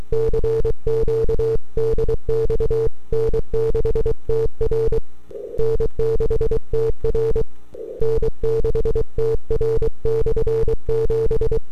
Morse code, heard as a keyed tone, about 23 wpm, 450 Hz. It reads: CQ DX N6TR N6TR N6TR/7
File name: n6tr / 12Oct2009-160